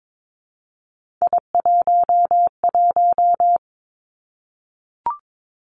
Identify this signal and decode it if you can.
Morse code, heard as a keyed tone, about 22 wpm, 700 Hz. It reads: I11
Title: Three Challenging Letter-Number Combinations 22wpm